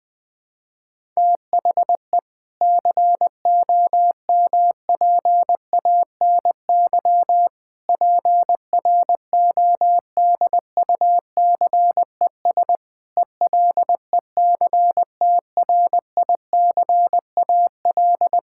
Morse code, heard as a keyed tone, about 20 wpm, 700 Hz. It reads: THE COMPANY PRODUCES ELECTRICAL
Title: Sentences from Top 1000 Words - Review Part 6 - 20wpm